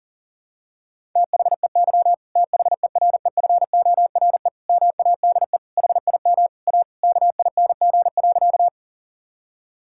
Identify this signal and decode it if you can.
Morse code, heard as a keyed tone, about 40 wpm, 700 Hz. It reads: THEY THEREFORE MADE HIM A KING.
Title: Sentences from Top 1000 Words - Review Part 5 - 40wpm